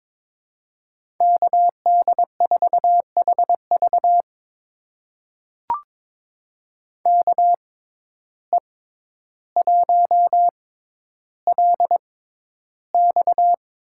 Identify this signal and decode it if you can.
Morse code, heard as a keyed tone, about 22 wpm, spaced as for 11 wpm, 700 Hz. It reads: KD4HV KE1LX
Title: USA Call Signs - 5 Characters - 6x Character Spacing - 22wpm